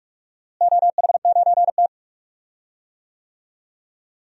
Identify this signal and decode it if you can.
Morse code, heard as a keyed tone, about 46 wpm, 700 Hz. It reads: OH9T
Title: DXCC-100 Call Signs - 4 Characters - Repeated 2x - 45wpm